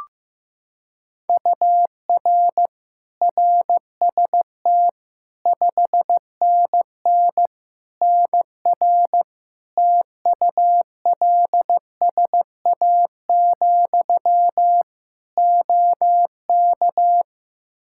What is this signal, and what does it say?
Morse code, heard as a keyed tone, about 15 wpm, 700 Hz. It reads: UR RST 5NN NR TULSA, OK
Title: Practice QSO - Exchange 1 - Beginning - 15wpm